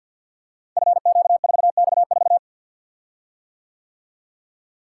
Morse code, heard as a keyed tone, about 51 wpm, 700 Hz. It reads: FQ4X4
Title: Code Groups - 5 Letter:Numbers - 50wpm